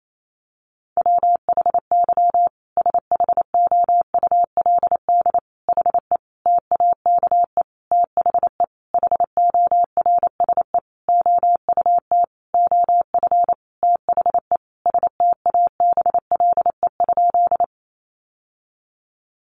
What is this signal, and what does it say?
Morse code, heard as a keyed tone, about 28 wpm, 700 Hz. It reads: WHY SHOULD HE TAKE THE HORSE OUT OF THE STABLE?
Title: Sentences from Top 1500 Words 28wpm